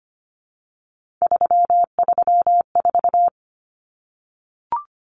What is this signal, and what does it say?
Morse code, heard as a keyed tone, about 25 wpm, 700 Hz. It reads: 334
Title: Numbers - 3-Digit 25wpm